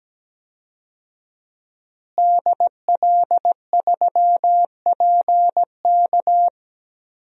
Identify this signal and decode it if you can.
Morse code, heard as a keyed tone, about 17 wpm, 700 Hz. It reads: DL3PK
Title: DXCC-100 Call Signs - 5 Characters - Repeated 2x - 17wpm